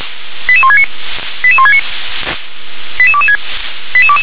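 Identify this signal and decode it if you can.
Five-tone selective call (ZVEI), heard as quick runs of five tones, tones of about 70 ms. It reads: 88169 88169 88226
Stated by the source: ZVEI five-tone selective call received by SDR.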